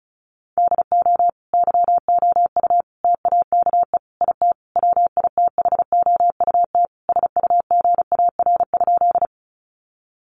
Morse code, heard as a keyed tone, about 34 wpm, 700 Hz. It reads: DO YOU TAKE IT WITHOUT SUGAR?
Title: Sentences from Top 1200 Words 35wpm